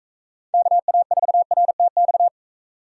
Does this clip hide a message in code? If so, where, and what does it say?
Yes: Morse code from 0.5 to 2.3 s, heard as a keyed tone, about 40 wpm, 700 Hz. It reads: KA4RTX